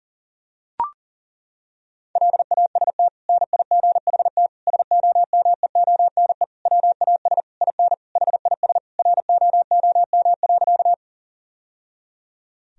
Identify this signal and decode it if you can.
Morse code, heard as a keyed tone, about 40 wpm, 700 Hz. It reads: LAST NIGHT SOMEONE WAS IN HIS ROOM.
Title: Sentences from Top 1000 Words - Review Part 4 - 40wpm